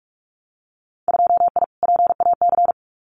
Morse code, heard as a keyed tone, about 46 wpm, 690 Hz. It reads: 2I PAC
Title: Winter Field Day Exchanges Encore 45wpm